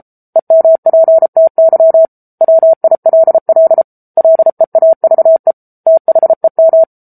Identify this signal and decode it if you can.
Morse code, heard as a keyed tone, about 34 wpm, 650 Hz. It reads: EMPTY WILL LEAVE THEM